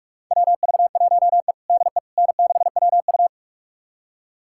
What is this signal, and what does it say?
Morse code, heard as a keyed tone, about 47 wpm, 690 Hz. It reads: WV1E DE N6WU
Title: USA Callsign DE Callsign - 4 Characters - 45wpm